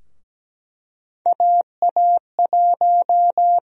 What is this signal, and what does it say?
Morse code, heard as a keyed tone, about 17 wpm, 700 Hz. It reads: AA1
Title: USA Call Sign Prefixes - 17wpm